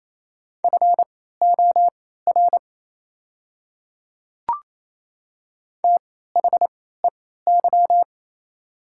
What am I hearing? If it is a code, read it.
Morse code, heard as a keyed tone, about 28 wpm, spaced as for 16 wpm, 690 Hz. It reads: FOR THEY
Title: 3x Character Spacing - Top 100 Words - 28wpm